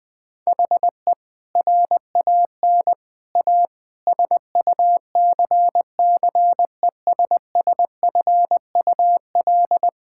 Morse code, heard as a keyed tone, about 20 wpm, 700 Hz. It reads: HE RAN A SUCCESSFUL
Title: Sentences from Top 1000 Words - Review Part 6 - 20wpm